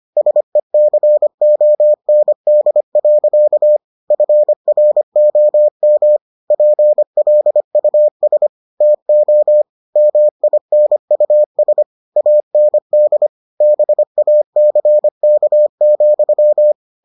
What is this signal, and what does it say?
Morse code, heard as a keyed tone, about 25 wpm, 600 Hz. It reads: SECOND. FROM PLUS TO MINUS AND BACK,